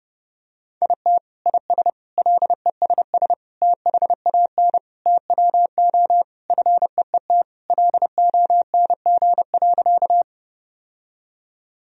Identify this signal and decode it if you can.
Morse code, heard as a keyed tone, about 30 wpm, 700 Hz. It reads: IT IS LESS THAN TWO FEET LONG.
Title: Sentences from Top 1000 Words - Review Part 2 - 30wpm